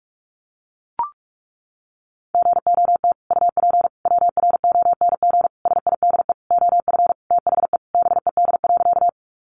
Morse code, heard as a keyed tone, about 44 wpm, 700 Hz. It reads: GOT UP WRONG SIDE OF THE BED.
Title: Sentences from Top 900 Words 45wpm